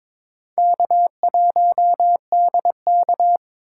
Morse code, heard as a keyed tone, about 22 wpm, 700 Hz. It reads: K1DK
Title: USA Call Signs - 4 Characters - 2x Character Spacing - 22wpm